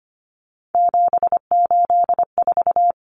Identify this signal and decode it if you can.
Morse code, heard as a keyed tone, about 25 wpm, 700 Hz. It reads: 784
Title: Numbers - 3-Digit 25wpm